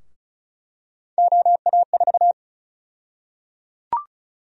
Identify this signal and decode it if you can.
Morse code, heard as a keyed tone, about 36 wpm, 700 Hz. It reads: OA4
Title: USA Call Sign Prefixes - Repeated 2x - 35wpm